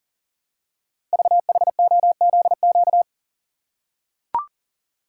Morse code, heard as a keyed tone, about 40 wpm, 700 Hz. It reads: VHOZQ
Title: Code Groups - 5 Letters - 40wpm